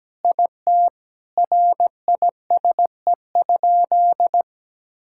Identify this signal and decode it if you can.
Morse code, heard as a keyed tone, about 17 wpm, 700 Hz. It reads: IT RISE?
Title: Sentences from Top 1000 Words - Review Part 6 - 17wpm